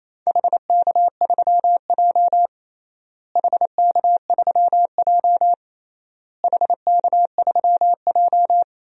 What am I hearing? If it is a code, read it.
Morse code, heard as a keyed tone, about 28 wpm, 700 Hz. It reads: HK3J HK3J HK3J
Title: DXCC-100 Call Signs - 4 Characters - Repeated 3x - 28wpm